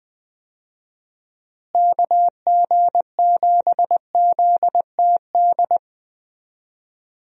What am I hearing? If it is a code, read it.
Morse code, heard as a keyed tone, about 19 wpm, 700 Hz. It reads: KG7ZTD
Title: USA Call Signs - 6 Characters - Repeated 3x - 20wpm